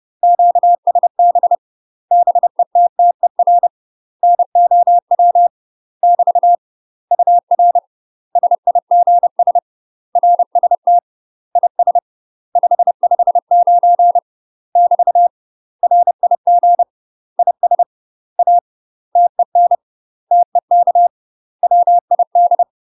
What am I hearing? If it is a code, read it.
Morse code, heard as a keyed tone, about 30 wpm, 700 Hz. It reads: QSB BETTER NOW <BT> UR SIGS RST IS 559 <BT> RIG IS A TEN TEK WID
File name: UK MASTER QSOs TRAINING / 30WPM / UK MASTER QSOs 0014